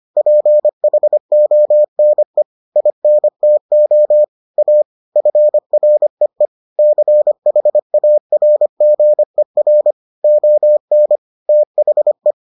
Morse code, heard as a keyed tone, about 25 wpm, 600 Hz. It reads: PHONE INTO A FREE CHARGER ON THE